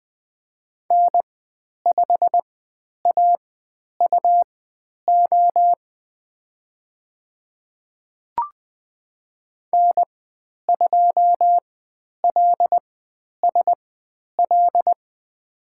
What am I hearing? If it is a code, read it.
Morse code, heard as a keyed tone, about 20 wpm, spaced as for 10 wpm, 700 Hz. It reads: N5AUO N2LSL
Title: 109 Lesson 18g - Letter m - Call Signs 20wpm